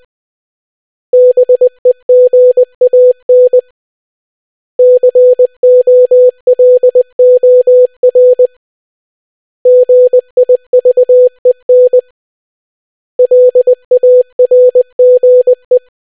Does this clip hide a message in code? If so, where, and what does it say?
Yes: Morse code from 1.1 to 15.8 s, heard as a keyed tone, about 20 wpm, 500 Hz. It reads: BEGAN COLOR GIVEN LARGE